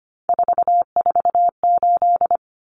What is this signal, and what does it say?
Morse code, heard as a keyed tone, about 25 wpm, 700 Hz. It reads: 448
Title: Common Cut Numbers - 3-Digit 25wpm